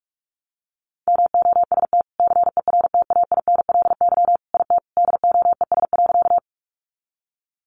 Morse code, heard as a keyed tone, about 44 wpm, 700 Hz. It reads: MOST CERTAINLY IT DOES.
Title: Sentences from Top 900 Words 45wpm